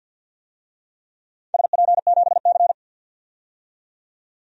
Morse code, heard as a keyed tone, about 48 wpm, 700 Hz. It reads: SP7C
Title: DXCC-100 Call Signs - 4 Characters - 50wpm